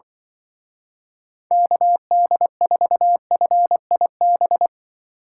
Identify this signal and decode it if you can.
Morse code, heard as a keyed tone, about 25 wpm, 700 Hz. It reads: KD4FIB